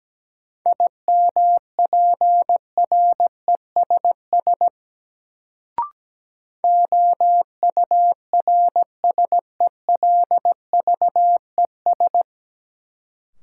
Morse code, heard as a keyed tone, about 17 wpm, 700 Hz. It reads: IMPRESS OURSELVES
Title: Sentences from Top 1600 Words - New Word Drill - 17wpm